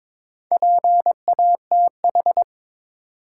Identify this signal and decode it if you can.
Morse code, heard as a keyed tone, about 22 wpm, 700 Hz. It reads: PATH
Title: Sentences from Top 1400 Words - New Word Drill Alt - 22wpm